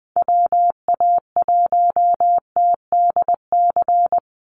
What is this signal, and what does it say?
Morse code, heard as a keyed tone, about 20 wpm, 700 Hz. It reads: WA1TDC
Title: USA Call Signs 20wpm